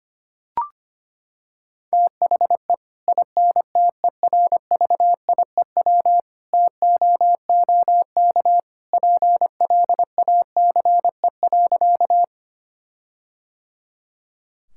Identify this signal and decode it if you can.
Morse code, heard as a keyed tone, about 25 wpm, 700 Hz. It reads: THE INTERVIEW TOOK PLACE.
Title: Sentences from Top 1600 Words 25wpm